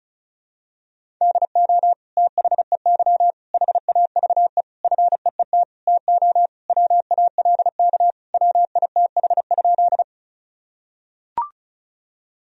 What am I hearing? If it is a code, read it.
Morse code, heard as a keyed tone, about 34 wpm, 700 Hz. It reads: DO THEY HAVE FEET TO WALK WITH?
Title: Sentences from Top 1000 Words - Review Part 1 - 35wpm